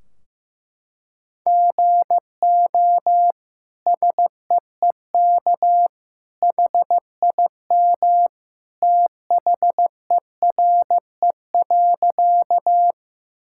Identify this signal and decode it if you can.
Morse code, heard as a keyed tone, about 15 wpm, 700 Hz. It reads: GO SEEK HIM THERE.
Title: Sentences from Top 1600 Words 15wpm